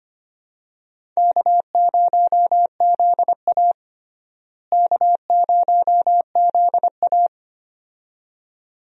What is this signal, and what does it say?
Morse code, heard as a keyed tone, about 25 wpm, 700 Hz. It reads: K0ZA K0ZA
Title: USA Call Signs - 4 Characters - Repeated 2x - 25wpm